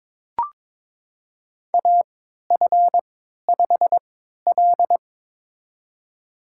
Morse code, heard as a keyed tone, about 22 wpm, spaced as for 13 wpm, 700 Hz. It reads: AF5L
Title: USA Call Signs - 4 Characters - 3x Character Spacing - 22wpm